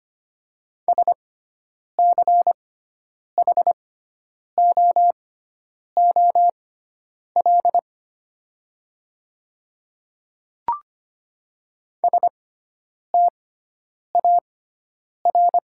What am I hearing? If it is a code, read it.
Morse code, heard as a keyed tone, about 25 wpm, spaced as for 9 wpm, 700 Hz. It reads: SCHOOL STAR
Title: 6x Character Spacing - Top 500 Words - 25wpm